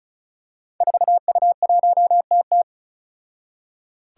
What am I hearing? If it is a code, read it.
Morse code, heard as a keyed tone, about 34 wpm, 700 Hz. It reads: 4U1TT
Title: Five Challenging Letter-Number Combinations 35wpm